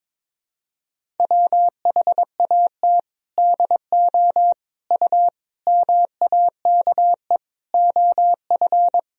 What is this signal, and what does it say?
Morse code, heard as a keyed tone, about 22 wpm, 700 Hz. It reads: WHAT DO U MAKE OF
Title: Sentences from Top 100 Words and Abbreviations 22wpm